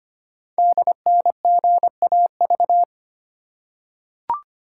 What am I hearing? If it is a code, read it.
Morse code, heard as a keyed tone, about 25 wpm, 700 Hz. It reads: DNGAV
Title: Code Groups - 5 Letters - 25wpm